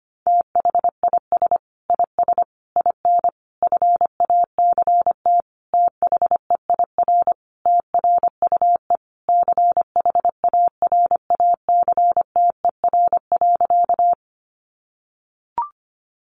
Morse code, heard as a keyed tone, about 25 wpm, 700 Hz. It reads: THIS IS IN FACT THEIR TRUE CHARACTER.
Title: Sentences from Top 1200 Words 25wpm